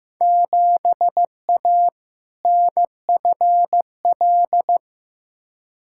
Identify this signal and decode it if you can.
Morse code, heard as a keyed tone, about 15 wpm, 700 Hz. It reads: 7A NFL
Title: Field Day Exchanges Encore - 15wpm